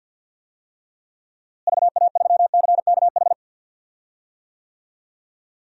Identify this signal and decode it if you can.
Morse code, heard as a keyed tone, about 49 wpm, 700 Hz. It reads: VA3CKH
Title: DXCC-100 Call Signs - 6 Characters - Repeated 3x - 50wpm